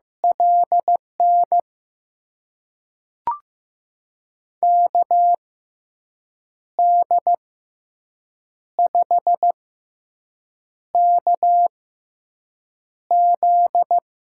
Morse code, heard as a keyed tone, about 15 wpm, spaced as for 6 wpm, 700 Hz. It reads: LN KD5KZ
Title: USA Call Signs - 6x Character Spacing - 15wpm